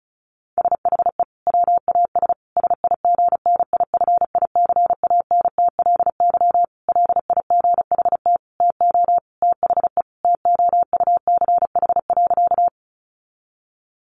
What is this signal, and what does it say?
Morse code, heard as a keyed tone, about 34 wpm, 700 Hz. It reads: SHE WAS SIGNIFICANTLY LIGHT TO THE TOUCH.
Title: Sentences from Top 1500 Words 35wpm